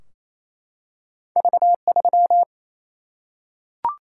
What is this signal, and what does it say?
Morse code, heard as a keyed tone, about 28 wpm, 700 Hz. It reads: V 3
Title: Single-Letter Single-Number 28wpm